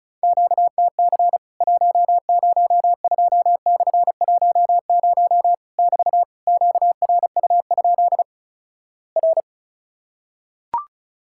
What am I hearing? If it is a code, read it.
Morse code, heard as a keyed tone, about 36 wpm, 700 Hz. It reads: QTC 102/10 <BT> QRU? R
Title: Worked All Europe DX Contest - QTC Exchange - 35wpm